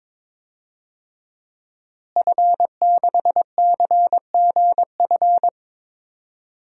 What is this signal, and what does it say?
Morse code, heard as a keyed tone, about 22 wpm, 700 Hz. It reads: F6CGF
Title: DXCC-100 Call Signs - 5 Characters - Repeated 2x - 22wpm